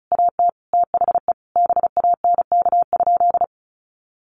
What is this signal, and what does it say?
Morse code, heard as a keyed tone, about 34 wpm, 700 Hz. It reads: AT THE BANK?
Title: Sentences from Top 1300 Words 35wpm